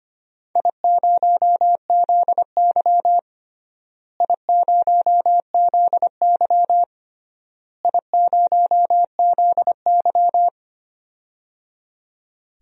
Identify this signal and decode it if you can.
Morse code, heard as a keyed tone, about 25 wpm, 700 Hz. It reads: I0ZY I0ZY I0ZY
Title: DXCC-100 Call Signs - 4 Characters - Repeated 3x - 25wpm